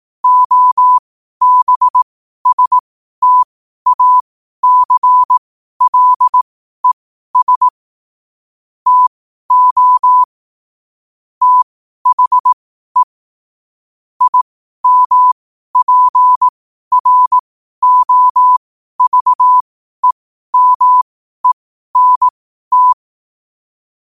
Morse code, heard as a keyed tone, about 18 wpm, spaced as for 12 wpm, 1000 Hz. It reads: OBSTACLES TO THE IMPROVEMENT
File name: qotdfiles / 12wpm